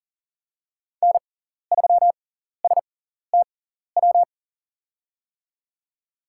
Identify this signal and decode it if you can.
Morse code, heard as a keyed tone, about 40 wpm, spaced as for 14 wpm, 700 Hz. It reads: N3STW
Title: USA Call Signs - 6x Character Spacing - 40wpm